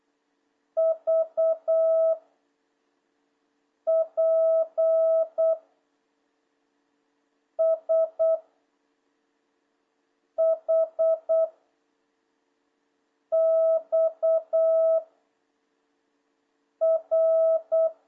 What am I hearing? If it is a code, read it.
Morse code, heard as a keyed tone, about 8 wpm, 650 Hz. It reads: V P S H X R
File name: jh8cbh / 2026cw